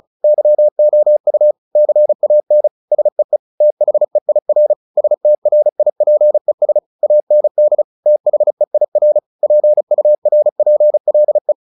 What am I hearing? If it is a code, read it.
Morse code, heard as a keyed tone, about 34 wpm, 600 Hz. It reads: YOU CAN SEE THEIR STRIPES AND THEIR PURPLE
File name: the old man and the sea 2 35wpm